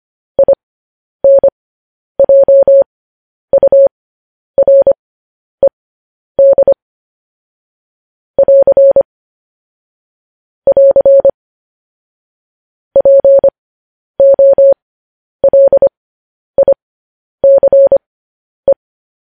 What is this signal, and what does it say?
Morse code, heard as a keyed tone, about 25 wpm, spaced as for 10 wpm, 550 Hz. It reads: INJURED <AR> <AR> POLICE
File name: podcast / News headlines-1x-25x10